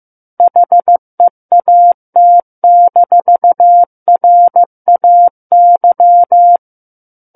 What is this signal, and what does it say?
Morse code, heard as a keyed tone, about 15 wpm, 700 Hz. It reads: HEAT-RAY
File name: Book One / 06-Chapter 6 - 15 WPM